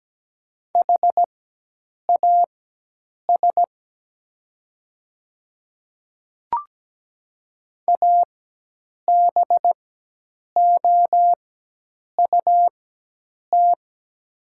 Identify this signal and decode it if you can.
Morse code, heard as a keyed tone, about 17 wpm, spaced as for 8 wpm, 700 Hz. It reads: HAS ABOUT
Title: 4x Character Spacing - Top 200 Words - 17wpm